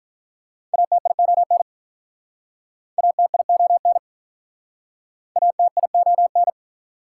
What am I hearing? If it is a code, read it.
Morse code, heard as a keyed tone, about 48 wpm, 700 Hz. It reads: ATION ATION ATION